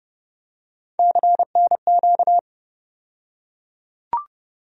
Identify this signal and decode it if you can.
Morse code, heard as a keyed tone, about 30 wpm, 700 Hz. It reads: CNQ
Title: USA Call Sign Postfixes - 3 Letters 30wpm